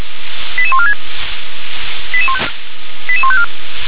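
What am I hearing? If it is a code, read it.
Five-tone selective call (ZVEI), heard as quick runs of five tones, tones of about 70 ms. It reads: 88146 88154
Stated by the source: ZVEI five-tone selective call received on SDR.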